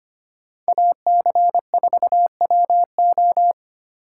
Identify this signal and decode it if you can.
Morse code, heard as a keyed tone, about 31 wpm, 700 Hz. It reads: AC4WO
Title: Speed Racing - USA 2x2 Call Signs 25wpm